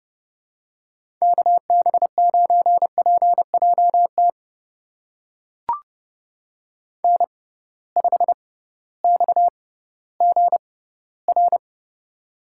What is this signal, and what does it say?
Morse code, heard as a keyed tone, about 30 wpm, spaced as for 16 wpm, 700 Hz. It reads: KB9PJT N5XGR
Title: USA Call Signs - 6x Character Spacing - 30wpm